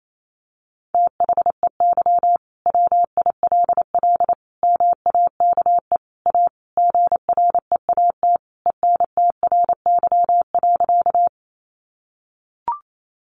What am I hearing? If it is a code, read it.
Morse code, heard as a keyed tone, about 28 wpm, 700 Hz. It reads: THEY WILL MAKE A GREAT ENTRY.
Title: Sentences from Top 1500 Words 28wpm